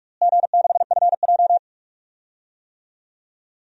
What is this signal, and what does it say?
Morse code, heard as a keyed tone, about 46 wpm, 700 Hz. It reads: G6FJ
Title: DXCC-100 Call Signs - 4 Characters - Repeated 3x - 45wpm